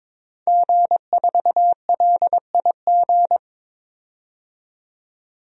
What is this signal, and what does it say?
Morse code, heard as a keyed tone, about 22 wpm, 700 Hz. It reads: G4LIG
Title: DXCC-100 Call Signs - 5 Characters - 22wpm